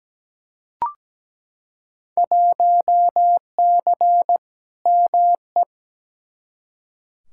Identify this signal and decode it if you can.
Morse code, heard as a keyed tone, about 17 wpm, 700 Hz. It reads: 1C ME
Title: Field Day Exchanges - 17wpm